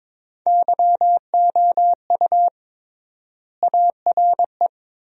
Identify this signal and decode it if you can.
Morse code, heard as a keyed tone, about 21 wpm, 700 Hz. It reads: YOU ARE
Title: 3x Word-Spacing - Sentences from Top 200 Words - 22wpm